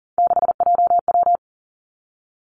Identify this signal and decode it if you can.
Morse code, heard as a keyed tone, about 41 wpm, 700 Hz. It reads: 6JW
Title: Three Challenging Letter-Number Combinations 40wpm